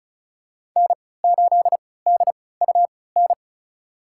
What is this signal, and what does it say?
Morse code, heard as a keyed tone, about 36 wpm, spaced as for 20 wpm, 690 Hz. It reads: N8DUN
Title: USA Call Signs - 5 Characters - 3x Character Spacing - 35wpm